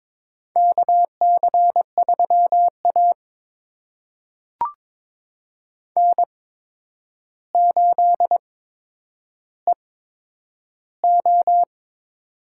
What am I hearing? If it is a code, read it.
Morse code, heard as a keyed tone, about 22 wpm, spaced as for 9 wpm, 700 Hz. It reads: KC3A N8EO
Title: USA Call Signs - 8x Character Spacing - 22wpm